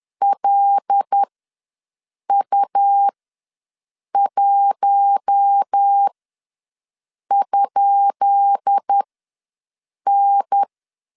Morse code, heard as a keyed tone, about 10 wpm, spaced as for 6 wpm, 800 Hz. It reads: LU1?N
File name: LEZIONI / 10 SIGNS / 10B INT ws 10wpm 4134